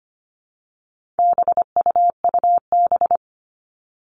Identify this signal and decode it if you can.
Morse code, heard as a keyed tone, about 25 wpm, 700 Hz. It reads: BUUB
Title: All Cut Numbers - 4-Digit 25wpm